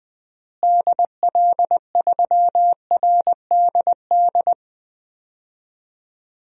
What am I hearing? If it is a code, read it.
Morse code, heard as a keyed tone, about 20 wpm, 700 Hz. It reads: DL3RDD
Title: DXCC-100 Call Signs - 6 Characters - Repeated 3x - 20wpm